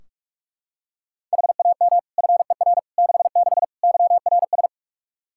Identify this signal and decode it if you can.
Morse code, heard as a keyed tone, about 46 wpm, 700 Hz. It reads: HAM FER 66 YRS
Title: K1USN SST QSO Practice 45wpm